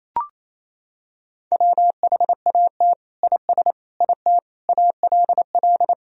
Morse code, heard as a keyed tone, about 28 wpm, 700 Hz. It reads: WHAT IS IT ALL
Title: Sentences from Top 100 Words and Abbreviations 28wpm